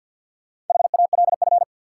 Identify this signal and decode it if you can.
Morse code, heard as a keyed tone, about 53 wpm, 700 Hz. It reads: HALF
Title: Sets of 3 Words - New Word Drill - 50wpm